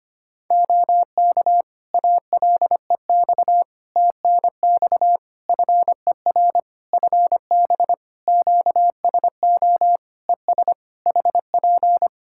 Morse code, heard as a keyed tone, about 25 wpm, 700 Hz. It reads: OK ALEX TNX FER FB QSO ES HP
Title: Practice QSO - Exchange 4 - Ending - 25wpm